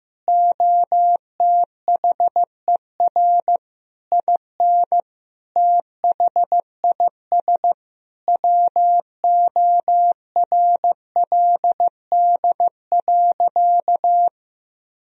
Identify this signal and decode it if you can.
Morse code, heard as a keyed tone, about 15 wpm, 700 Hz. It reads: OTHER IN THIS WORLD.
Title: Sentences from Top 1000 Words - Review Part 1 - 15wpm